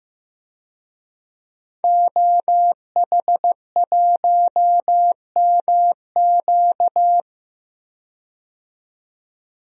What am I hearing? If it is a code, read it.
Morse code, heard as a keyed tone, about 15 wpm, 700 Hz. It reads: OH1MQ
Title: DXCC-100 Call Signs - 5 Characters - 15wpm